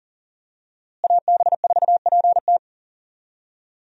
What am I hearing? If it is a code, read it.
Morse code, heard as a keyed tone, about 40 wpm, 700 Hz. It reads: AB4PT
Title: USA Call Signs - 5 Characters - Repeated 3x - 40wpm